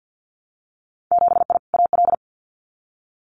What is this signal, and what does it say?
Morse code, heard as a keyed tone, about 50 wpm, 700 Hz. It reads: 7I AL
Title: Winter Field Day Exchanges Encore 50wpm